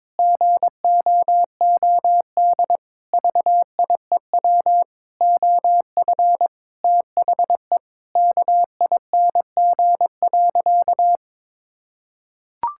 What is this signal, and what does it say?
Morse code, heard as a keyed tone, about 22 wpm, 700 Hz. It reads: GOOD VIEW OF THE KING.
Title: Sentences from Top 1000 Words - Review Part 6 - 22wpm